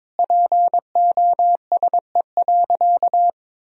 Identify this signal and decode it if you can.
Morse code, heard as a keyed tone, about 22 wpm, 700 Hz. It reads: POSE.
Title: Sentences from Top 1000 Words - Review Part 2 - 22wpm